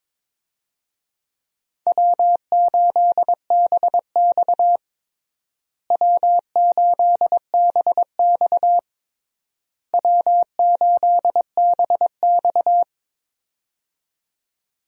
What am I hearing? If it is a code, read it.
Morse code, heard as a keyed tone, about 22 wpm, 700 Hz. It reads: W8BX W8BX W8BX
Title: USA Call Signs - 4 Characters - Repeated 3x - 22wpm